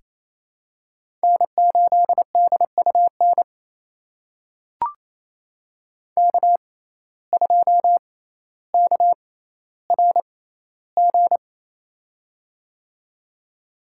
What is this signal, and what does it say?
Morse code, heard as a keyed tone, about 28 wpm, spaced as for 14 wpm, 700 Hz. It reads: N8DUN K2KRG
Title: USA Call Signs - 5 Characters - 6x Character Spacing - 28wpm